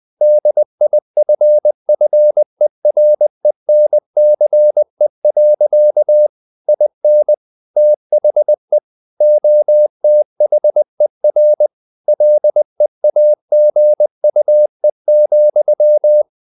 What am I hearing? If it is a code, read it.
Morse code, heard as a keyed tone, about 20 wpm, 600 Hz. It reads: DIFFERENCE. IN THE OTHER LEAGUE,
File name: the old man and the sea / the old man and the sea part 1 20wpm